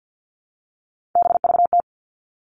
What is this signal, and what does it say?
Morse code, heard as a keyed tone, about 50 wpm, 700 Hz. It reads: BVT
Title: All Cut Numbers - 3-Digit 50wpm